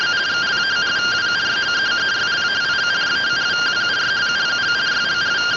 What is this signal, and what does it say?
Two-tone teleprinter signal, about 170 Hz shift.